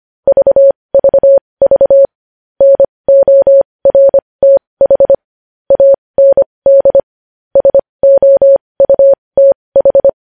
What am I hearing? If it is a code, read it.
Morse code, heard as a keyed tone, about 25 wpm, spaced as for 20 wpm, 550 Hz. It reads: VVV NORTH AND SOUTH
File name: podcast / News headlines-1x-25x20